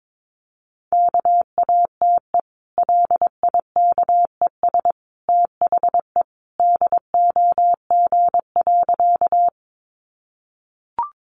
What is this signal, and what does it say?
Morse code, heard as a keyed tone, about 22 wpm, 700 Hz. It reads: KATE LIKES THE DOG.
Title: Easy Phrases 22wpm